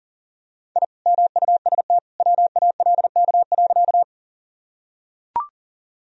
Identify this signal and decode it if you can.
Morse code, heard as a keyed tone, about 40 wpm, 700 Hz. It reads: I MUST WALK.
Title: Sentences from Top 1000 Words - Review Part 1 - 40wpm